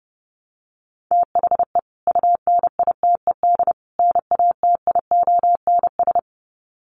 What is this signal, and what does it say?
Morse code, heard as a keyed tone, about 30 wpm, 700 Hz. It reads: THE UNITED NATIONS
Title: Sets of 3 Words 30wpm